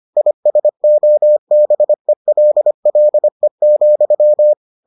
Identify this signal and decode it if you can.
Morse code, heard as a keyed tone, about 25 wpm, 600 Hz. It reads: ISOBELLE,